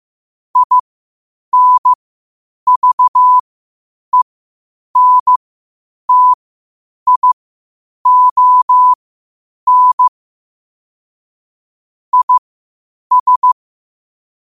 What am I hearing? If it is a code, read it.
Morse code, heard as a keyed tone, about 15 wpm, spaced as for 8 wpm, 1000 Hz. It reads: INVENTION IS
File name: qotdfiles / 8wpm